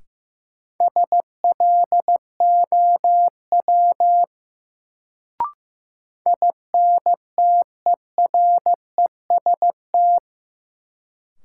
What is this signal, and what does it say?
Morse code, heard as a keyed tone, about 15 wpm, 700 Hz. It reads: SLOW INTEREST
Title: Sentences from Top 400 Words - New Word Drill - 15wpm